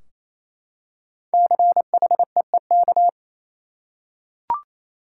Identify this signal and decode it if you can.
Morse code, heard as a keyed tone, about 28 wpm, 700 Hz. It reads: CHEEK
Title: Five Letter Words - 28wpm